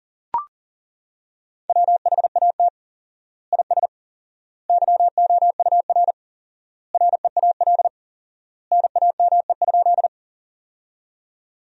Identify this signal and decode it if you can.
Morse code, heard as a keyed tone, about 40 wpm, 700 Hz. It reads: WHAT IS YOUR REAL NAME?
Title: 4x Word-Spacing - Sentences from Top 300 Words - 40wpm